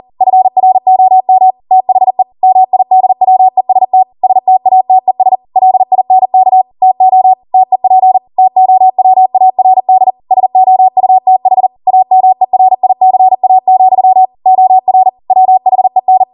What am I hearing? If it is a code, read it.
Morse code, heard as a keyed tone, about 40 wpm, 750 Hz. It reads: FROM THE MIDWEST STATES LINK TO TEP TOWARD SOUTH AMERICA, OR WHEN